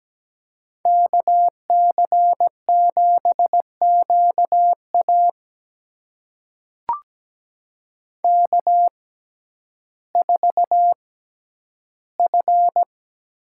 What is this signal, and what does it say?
Morse code, heard as a keyed tone, about 17 wpm, spaced as for 10 wpm, 700 Hz. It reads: KC7QA K4F
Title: USA Call Signs - 5 Characters - 6x Character Spacing - 17wpm